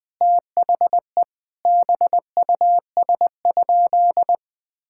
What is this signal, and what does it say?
Morse code, heard as a keyed tone, about 20 wpm, 700 Hz. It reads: THE BUS?
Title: Sentences from Top 1000 Words - Review Part 6 - 20wpm